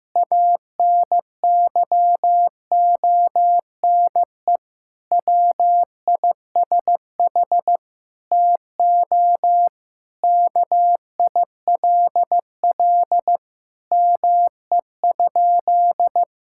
Morse code, heard as a keyed tone, about 15 wpm, 700 Hz. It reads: ANYONE WISH TO KILL ME?
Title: Sentences from Top 1600 Words 15wpm